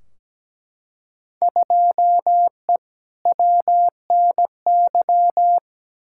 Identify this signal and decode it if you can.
Morse code, heard as a keyed tone, about 17 wpm, 700 Hz. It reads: 2E WNY
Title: Field Day Exchanges Encore - 17wpm